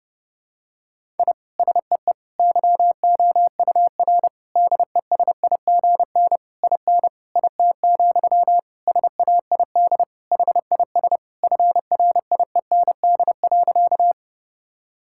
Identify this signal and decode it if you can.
Morse code, heard as a keyed tone, about 32 wpm, 700 Hz. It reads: I SEE YOUR DESIGN IN IT, SAID HIS FRIEND.
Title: Sentences from Top 1000 Words - Review Part 5 - 30wpm